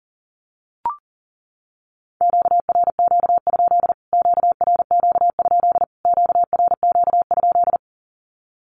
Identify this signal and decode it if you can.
Morse code, heard as a keyed tone, about 40 wpm, 700 Hz. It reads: QRQ? QRQ? QRQ?
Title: Q-Codes Repeated 3x 40wpm